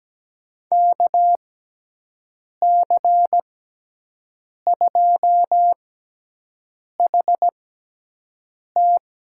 Morse code, heard as a keyed tone, about 17 wpm, spaced as for 6 wpm, 700 Hz. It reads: KC2HT
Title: USA Call Signs - 6 Characters - 6x Character Spacing - 17wpm